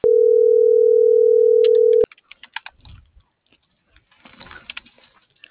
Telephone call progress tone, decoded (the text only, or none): ringback tone